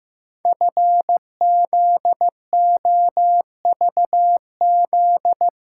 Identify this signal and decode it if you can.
Morse code, heard as a keyed tone, about 15 wpm, 700 Hz. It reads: FZOVZ
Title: Code Groups - 5 Letters - 15wpm